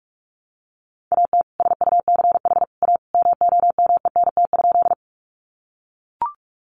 Morse code, heard as a keyed tone, about 44 wpm, 700 Hz. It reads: AT SUCH A MOMENT?
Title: Sentences from Top 1000 Words 45wpm